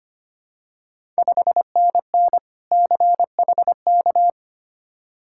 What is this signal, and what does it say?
Morse code, heard as a keyed tone, about 25 wpm, 700 Hz. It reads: 5NN CHK
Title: Ontario QSO Party Exchange 25wpm